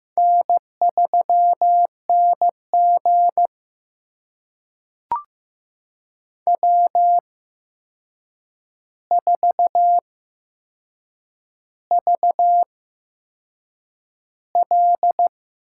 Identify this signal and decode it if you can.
Morse code, heard as a keyed tone, about 15 wpm, spaced as for 6 wpm, 700 Hz. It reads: N3NG W4VL
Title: USA Call Signs - 4 Characters - 8x Character Spacing - 15wpm